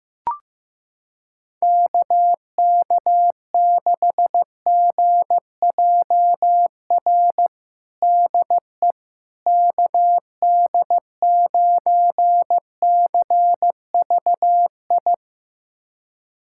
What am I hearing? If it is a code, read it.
Morse code, heard as a keyed tone, about 15 wpm, 700 Hz. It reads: KK6GJR DE KD9CVI
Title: USA Callsign DE Callsign - 6 Characters - 15wpm